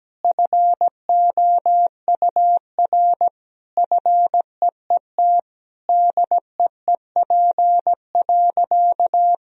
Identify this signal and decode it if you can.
Morse code, heard as a keyed tone, about 17 wpm, 700 Hz. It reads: FOUR FEET DEEP.
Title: Sentences from Top 1000 Words - Review Part 3 - 17wpm